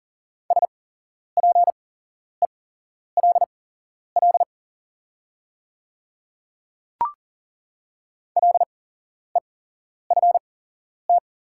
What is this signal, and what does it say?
Morse code, heard as a keyed tone, about 40 wpm, spaced as for 11 wpm, 690 Hz. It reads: SPELL LEFT
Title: 8x Character Spacing - Top 300 Words - 40wpm